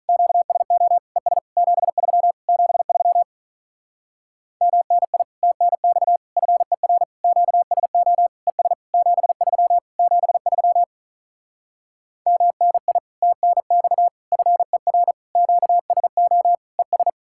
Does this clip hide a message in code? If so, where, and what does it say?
Yes: Morse code from 0.1 to 17.1 s, heard as a keyed tone, about 48 wpm, 700 Hz. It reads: QSO ES 73 73 MNI TNX FER QSO ES 73 73 MNI TNX FER QSO ES